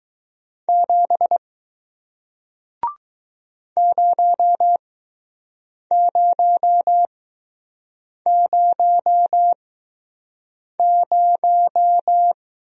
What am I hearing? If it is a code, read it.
Morse code, heard as a keyed tone, about 20 wpm, spaced as for 6 wpm, 700 Hz. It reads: 7 0000